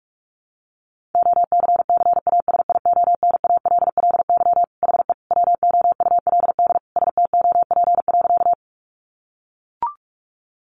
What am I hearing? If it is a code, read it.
Morse code, heard as a keyed tone, about 44 wpm, 700 Hz. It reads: OCCASIONALLY HE WOULD STOP.
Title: Sentences from Top 1300 Words 45wpm